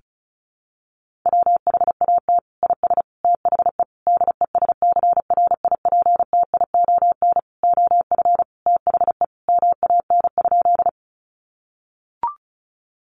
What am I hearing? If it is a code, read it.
Morse code, heard as a keyed tone, about 34 wpm, 700 Hz. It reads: WHAT IS THE DESCRIPTION OF THE MAN?
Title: Sentences from Top 1300 Words 35wpm